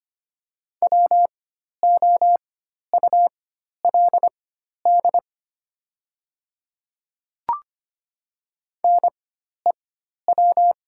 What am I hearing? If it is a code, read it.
Morse code, heard as a keyed tone, about 25 wpm, spaced as for 12 wpm, 700 Hz. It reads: WOULD NEW
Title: 4x Character Spacing - Top 200 Words - 25wpm